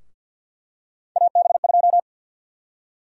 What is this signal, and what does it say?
Morse code, heard as a keyed tone, about 52 wpm, 700 Hz. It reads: AB2
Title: USA Call Sign Prefixes - 50wpm